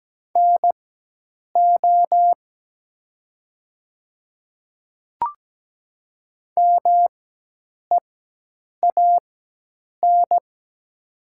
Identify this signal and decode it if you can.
Morse code, heard as a keyed tone, about 17 wpm, spaced as for 8 wpm, 700 Hz. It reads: NO MEAN
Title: 4x Character Spacing - Top 300 Words - 17wpm